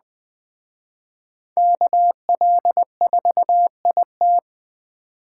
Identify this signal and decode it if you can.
Morse code, heard as a keyed tone, about 21 wpm, 700 Hz. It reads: KL4IT
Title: Speed Racing - USA 2x2 Call Signs 17wpm